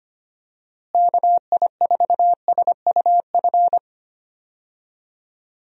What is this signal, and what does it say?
Morse code, heard as a keyed tone, about 25 wpm, 700 Hz. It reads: KI4SUF
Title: USA Call Signs - 6 Characters - Repeated 2x - 25wpm